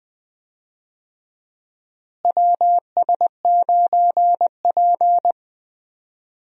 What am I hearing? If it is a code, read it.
Morse code, heard as a keyed tone, about 20 wpm, 700 Hz. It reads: WS9P
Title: USA Call Signs - 4 Characters - Repeated 2x - 20wpm